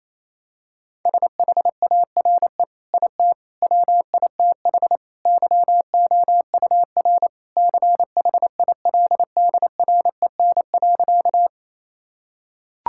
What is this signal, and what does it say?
Morse code, heard as a keyed tone, about 28 wpm, 700 Hz. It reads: SHARE IT WITH YOUR CHILDREN.
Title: Sentences from Top 1000 Words - Review Part 6 - 28wpm